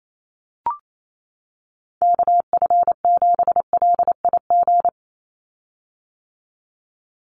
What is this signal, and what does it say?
Morse code, heard as a keyed tone, about 28 wpm, 700 Hz. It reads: KF7LIG
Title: USA Call Signs 28wpm